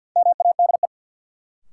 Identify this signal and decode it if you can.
Morse code, heard as a keyed tone, about 50 wpm, 700 Hz. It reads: MADE
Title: Sentences from Top 200 Words - New Word Drill - 50wpm